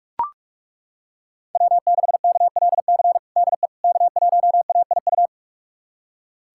Morse code, heard as a keyed tone, about 42 wpm, 700 Hz. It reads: W6KLC DE K1AIU
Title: USA Callsign DE Callsign - 5 Characters - 45wpm